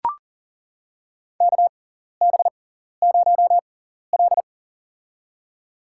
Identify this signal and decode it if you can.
Morse code, heard as a keyed tone, about 40 wpm, spaced as for 14 wpm, 700 Hz. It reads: KB0L
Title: USA Call Signs - 4 Characters - 6x Character Spacing - 40wpm